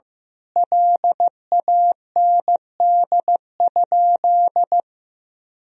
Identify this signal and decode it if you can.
Morse code, heard as a keyed tone, about 15 wpm, 700 Hz. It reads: LAND?
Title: Sentences from Top 1000 Words - Review Part 1 - 15wpm